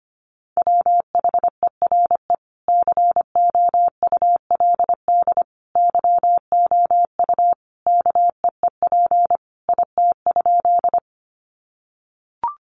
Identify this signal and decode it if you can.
Morse code, heard as a keyed tone, about 25 wpm, 700 Hz. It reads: WHERE COULD YOU KEEP IT?
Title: Sentences from Top 300 Words 25wpm